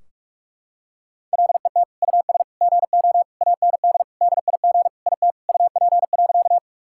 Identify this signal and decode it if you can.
Morse code, heard as a keyed tone, about 46 wpm, 700 Hz. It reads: LET US GO AND DIG IT UP.
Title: Sentences from Top 1600 Words 45wpm